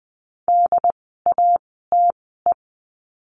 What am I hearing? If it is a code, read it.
Morse code, heard as a keyed tone, about 20 wpm, spaced as for 14 wpm, 700 Hz. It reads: DATE
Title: 068 Lesson 12f - Letter d - Words 20wpm